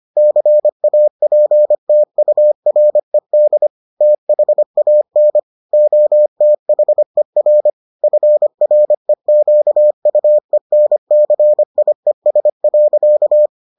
Morse code, heard as a keyed tone, about 25 wpm, 600 Hz. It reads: CAPTURED THAN OTHER FREQUENCIES.